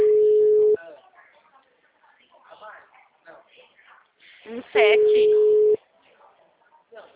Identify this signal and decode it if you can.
Ringback tone (Europe, 425 Hz), with speech.